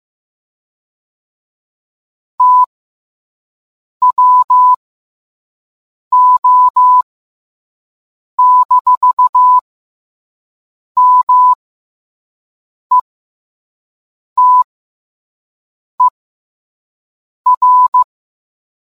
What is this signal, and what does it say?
Morse code, heard as a keyed tone, about 15 wpm, spaced as for 5 wpm, 1000 Hz. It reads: TWO-METER